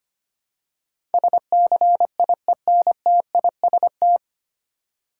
Morse code, heard as a keyed tone, about 25 wpm, 700 Hz. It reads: SCIENTIST
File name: practice / 25wpm / 9 Letter Words - First 3 Letters Given - 25wpm